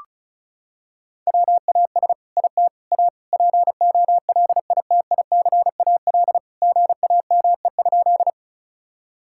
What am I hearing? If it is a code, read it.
Morse code, heard as a keyed tone, about 34 wpm, 700 Hz. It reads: WAS IT A POLITICAL GAME?
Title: Sentences from Top 1000 Words - Review Part 5 - 35wpm